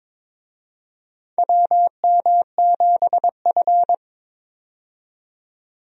Morse code, heard as a keyed tone, about 22 wpm, 700 Hz. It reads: WM7F
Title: USA Call Signs - 4 Characters - 22wpm